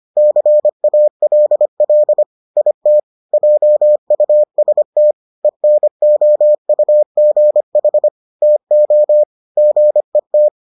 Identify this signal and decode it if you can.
Morse code, heard as a keyed tone, about 25 wpm, 600 Hz. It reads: CALL IT JUST ENOUGH TO GET